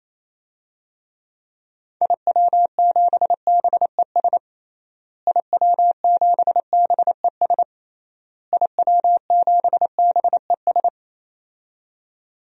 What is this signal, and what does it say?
Morse code, heard as a keyed tone, about 28 wpm, 700 Hz. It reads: IW7BES IW7BES IW7BES
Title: DXCC-100 Call Signs - 6 Characters - Repeated 3x - 28wpm